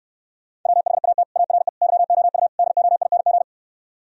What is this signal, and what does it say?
Morse code, heard as a keyed tone, about 67 wpm, 700 Hz. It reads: WHAT ARE YOU DOING